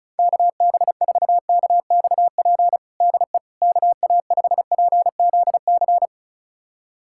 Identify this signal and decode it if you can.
Morse code, heard as a keyed tone, about 36 wpm, 700 Hz. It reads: KB4KXP DE KA5PZC
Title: USA Callsign DE Callsign - 6 Characters - 35wpm